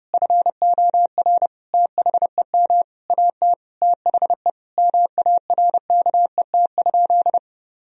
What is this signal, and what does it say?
Morse code, heard as a keyed tone, about 30 wpm, 700 Hz. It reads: FOR THEM AT THE MARKET?
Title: Sentences from Top 1000 Words - Review Part 3 - 30wpm